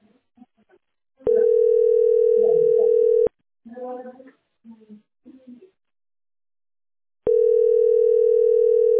Ringback tone (North America), 440 + 480 Hz.